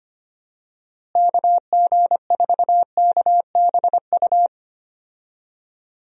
Morse code, heard as a keyed tone, about 25 wpm, 700 Hz. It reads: KG4KBU
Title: USA Call Signs - 6 Characters - Repeated 2x - 25wpm